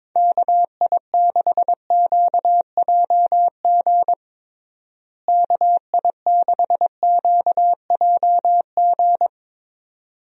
Morse code, heard as a keyed tone, about 22 wpm, 700 Hz. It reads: KI6QJG KI6QJG
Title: USA Call Signs - 6 Characters - Repeated 2x - 22wpm